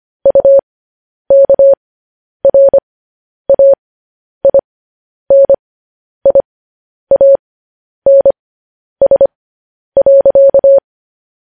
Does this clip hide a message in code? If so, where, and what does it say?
Yes: Morse code from 0.3 to 10.8 s, heard as a keyed tone, about 25 wpm, spaced as for 10 wpm, 550 Hz. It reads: UKRAINIANS.